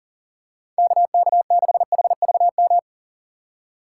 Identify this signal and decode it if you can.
Morse code, heard as a keyed tone, about 40 wpm, 700 Hz. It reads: KK6HVM
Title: USA Call Signs - 6 Characters - 3x Character Spacing - 40wpm